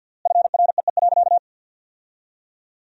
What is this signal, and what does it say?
Morse code, heard as a keyed tone, about 48 wpm, 700 Hz. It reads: FREE.
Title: 3x Word-Spacing - Sentences from Top 500 Words - 50wpm